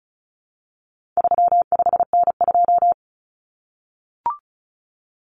Morse code, heard as a keyed tone, about 34 wpm, 700 Hz. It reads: 35N2
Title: Common Cut Numbers - 4-Digit 35wpm